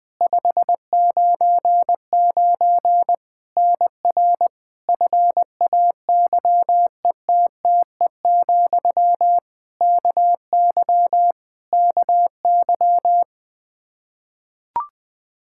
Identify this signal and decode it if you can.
Morse code, heard as a keyed tone, about 20 wpm, 700 Hz. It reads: 599 NR FAYETTE, KY KY
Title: Practice QSO - Exchange 1 - Beginning - 20wpm